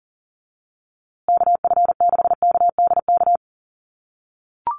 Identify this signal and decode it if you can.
Morse code, heard as a keyed tone, about 38 wpm, 700 Hz. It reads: KF6KDK
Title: USA Call Signs Repeated 3x 40wpm